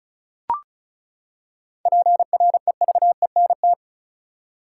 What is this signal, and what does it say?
Morse code, heard as a keyed tone, about 34 wpm, 700 Hz. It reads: PREVENT
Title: Seven Letter Words - 35wpm